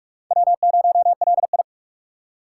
Morse code, heard as a keyed tone, about 48 wpm, 700 Hz. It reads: W0LI
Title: USA Call Signs - 4 Characters - 2x Character Spacing - 45wpm